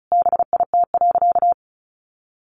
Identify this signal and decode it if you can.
Morse code, heard as a keyed tone, about 34 wpm, 700 Hz. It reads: BIT.
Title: Sentences from Top 900 Words 35wpm